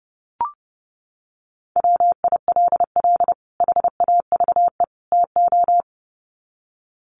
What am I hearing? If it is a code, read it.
Morse code, heard as a keyed tone, about 30 wpm, 700 Hz. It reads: WILL HAVE TO
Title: Sets of 3 Words 30wpm